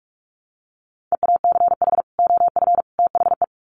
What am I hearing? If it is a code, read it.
Morse code, heard as a keyed tone, about 44 wpm, 700 Hz. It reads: EACH OF THE
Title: Sets of 3 Words 45wpm